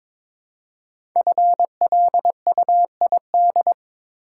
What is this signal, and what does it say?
Morse code, heard as a keyed tone, about 22 wpm, 700 Hz. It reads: FLUID
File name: 5 Letter Words - First 3 Letters Given - 22wpm